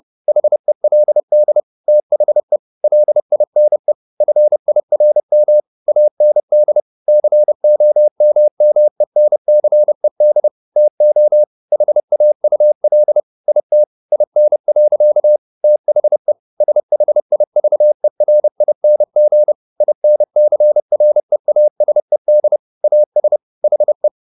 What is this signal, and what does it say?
Morse code, heard as a keyed tone, about 30 wpm, 600 Hz. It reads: HELD THE LINE FIRM AND COMMENCED TO HAUL IT IN. THE SHIVERING INCREASED AS HE
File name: the old man and the sea / the old man and the sea 1 30wpm